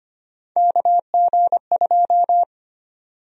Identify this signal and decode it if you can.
Morse code, heard as a keyed tone, about 25 wpm, 700 Hz. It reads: KG2
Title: USA Call Sign Prefixes - Repeated 2x - 25wpm